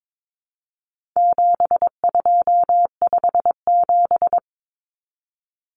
Morse code, heard as a keyed tone, about 22 wpm, 700 Hz. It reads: 7257
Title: Numbers - 4-Digit 22wpm